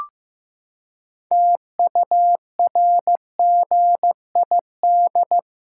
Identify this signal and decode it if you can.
Morse code, heard as a keyed tone, about 15 wpm, 700 Hz. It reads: TURGID
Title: Pangram Challenge v2 - 15wpm